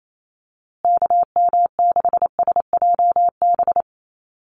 Morse code, heard as a keyed tone, about 28 wpm, 700 Hz. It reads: KM6SJB
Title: USA Call Signs Repeated 3x 28wpm